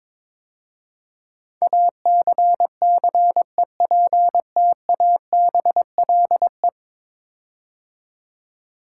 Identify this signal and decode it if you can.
Morse code, heard as a keyed tone, about 21 wpm, 700 Hz. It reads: ACCEPTABLE
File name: practice / 22wpm / 10 Letter Words - First 3 Letters Given - 22wpm